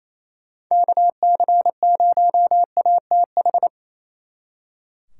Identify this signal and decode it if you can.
Morse code, heard as a keyed tone, about 34 wpm, 700 Hz. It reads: KC0ATH
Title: Speed Racing - USA 2x3 Call Signs 28wpm